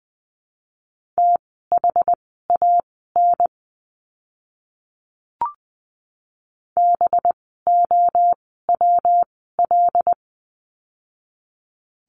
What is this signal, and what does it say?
Morse code, heard as a keyed tone, about 20 wpm, spaced as for 14 wpm, 700 Hz. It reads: THAN BOWL
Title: 185 Lesson 30f - Letter b - Words 20wpm